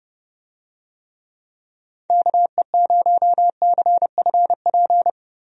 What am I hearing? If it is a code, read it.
Morse code, heard as a keyed tone, about 30 wpm, 700 Hz. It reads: KE0CFP
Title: USA Call Signs - 6 Characters - 30wpm